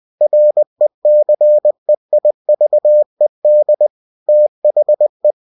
Morse code, heard as a keyed tone, about 20 wpm, 600 Hz. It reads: RECEIVED THE
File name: cw-books / winnie-the-pooh / winnie-20wpm-page-0119